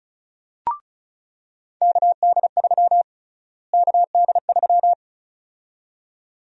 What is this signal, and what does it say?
Morse code, heard as a keyed tone, about 36 wpm, 700 Hz. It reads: KD3 KD3
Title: USA Call Sign Prefixes - Repeated 2x - 35wpm